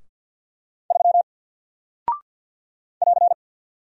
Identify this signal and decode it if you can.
Morse code, heard as a keyed tone, about 48 wpm, 700 Hz. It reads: 3 <AR>
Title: Single Character (Letter:Number:Punctuation:Prosign) Alt 50wpm